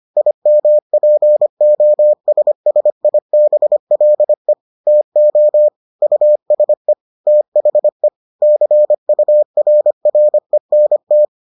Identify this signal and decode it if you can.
Morse code, heard as a keyed tone, about 25 wpm, 600 Hz. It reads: IMPOSSIBLE TO USE THE CURRENT